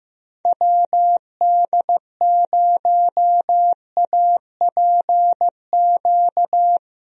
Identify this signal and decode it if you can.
Morse code, heard as a keyed tone, about 15 wpm, 700 Hz. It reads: WD0APQ
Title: USA Call Signs - 6 Characters - Repeated 3x - 15wpm